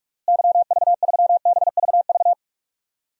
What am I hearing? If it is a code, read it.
Morse code, heard as a keyed tone, about 46 wpm, 700 Hz. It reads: YV3BVV
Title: DXCC-100 Call Signs - 6 Characters - Repeated 3x - 45wpm